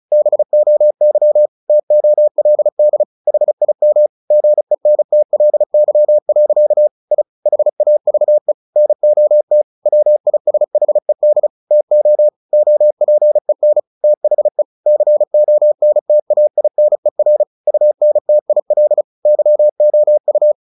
Morse code, heard as a keyed tone, about 34 wpm, 600 Hz. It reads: BOY TOLD HIM GENTLY. I HAVE NOT WISHED TO OPEN THE CONTAINER UNTIL YOU
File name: the old man and the sea / the old man and the sea 1 35wpm